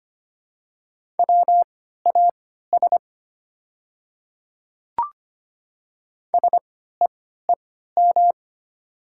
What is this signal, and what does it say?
Morse code, heard as a keyed tone, about 25 wpm, spaced as for 14 wpm, 700 Hz. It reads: WAS SEEM
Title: 3x Character Spacing - Top 300 Words - 25wpm